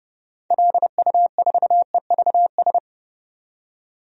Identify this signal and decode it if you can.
Morse code, heard as a keyed tone, about 30 wpm, 700 Hz. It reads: LU4EVS
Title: DXCC-100 Call Signs - 6 Characters - 30wpm